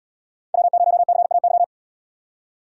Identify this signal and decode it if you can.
Morse code, heard as a keyed tone, about 77 wpm, 700 Hz. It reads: W0CAY